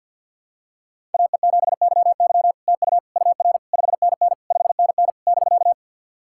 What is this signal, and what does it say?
Morse code, heard as a keyed tone, about 50 wpm, 700 Hz. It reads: AE7YY TU UR 5NN 5NN <BK>
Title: SOTA QSO Practice 50wpm